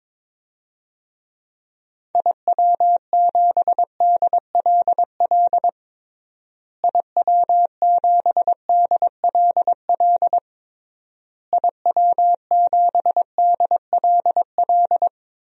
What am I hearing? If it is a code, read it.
Morse code, heard as a keyed tone, about 22 wpm, 700 Hz. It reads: IW7DLL IW7DLL IW7DLL
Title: DXCC-100 Call Signs - 6 Characters - Repeated 3x - 22wpm